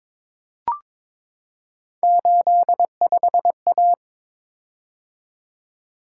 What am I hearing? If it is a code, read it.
Morse code, heard as a keyed tone, about 22 wpm, 700 Hz. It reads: 85A
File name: practice / 22wpm / Three Letter-Number Combinations - 22wpm